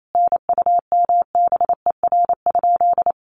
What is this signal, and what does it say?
Morse code, heard as a keyed tone, about 28 wpm, 700 Hz. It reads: NUMBER?
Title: Sentences from Top 100 Words 28wpm